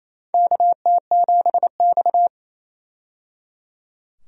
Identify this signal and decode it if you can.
Morse code, heard as a keyed tone, about 28 wpm, 700 Hz. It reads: KT7X
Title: USA Call Signs - 4 Characters - 28wpm